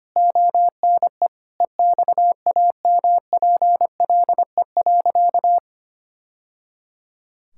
Morse code, heard as a keyed tone, about 25 wpm, 700 Hz. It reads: ONE EXAMPLE.
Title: Sentences from Top 1000 Words - Review Part 6 - 25wpm